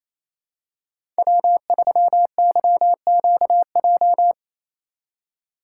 Morse code, heard as a keyed tone, about 28 wpm, 700 Hz. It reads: W3YQJ
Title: Code Groups - 5 Letter:Numbers - 28wpm